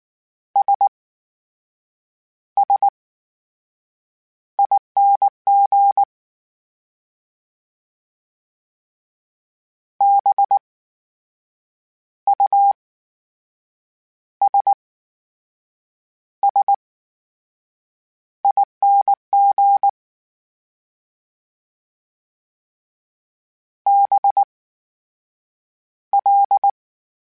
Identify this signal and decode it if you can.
Morse code, heard as a keyed tone, about 19 wpm, spaced as for 7 wpm, 800 Hz. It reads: SSING BUSSING BL
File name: Special INGs / Special INGs-mp3s / VerySlowToFaster Ing @18